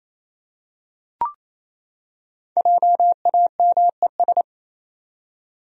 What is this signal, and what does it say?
Morse code, heard as a keyed tone, about 27 wpm, 700 Hz. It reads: JAMES
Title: USA Common Names - 28wpm